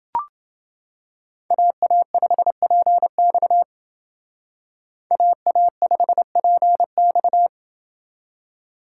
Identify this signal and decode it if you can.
Morse code, heard as a keyed tone, about 28 wpm, 700 Hz. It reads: AA5PX AA5PX